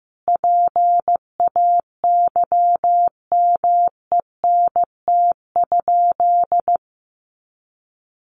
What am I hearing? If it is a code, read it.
Morse code, heard as a keyed tone, about 15 wpm, 700 Hz. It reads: PAYMENT?
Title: Sentences from Top 1400 Words 15wpm